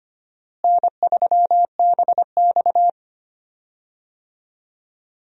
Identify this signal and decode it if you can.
Morse code, heard as a keyed tone, about 25 wpm, 700 Hz. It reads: N3BX
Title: USA Call Signs - 4 Characters - Repeated 2x - 25wpm